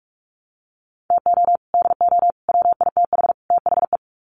Morse code, heard as a keyed tone, about 44 wpm, 700 Hz. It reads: TO DO WITH THE
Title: Sets of 4 Words 45wpm